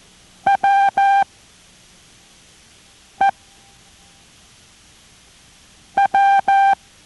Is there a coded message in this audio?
Yes: Morse code from 0.5 to 6.7 s, heard as a keyed tone, about 14 wpm, spaced as for 3 wpm, 790 Hz. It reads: WEW